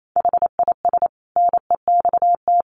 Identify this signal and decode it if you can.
Morse code, heard as a keyed tone, about 28 wpm, 700 Hz. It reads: HIS NEXT
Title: Sentences from Top 1300 Words 28wpm